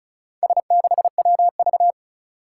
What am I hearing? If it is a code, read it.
Morse code, heard as a keyed tone, about 34 wpm, 700 Hz. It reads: S6WV
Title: Four Challenging Letter-Number Combinations 35wpm